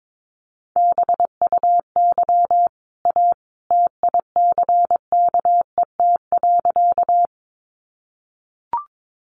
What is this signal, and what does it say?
Morse code, heard as a keyed tone, about 22 wpm, 700 Hz. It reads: BUY A TICKET.
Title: Sentences from Top 1400 Words 22wpm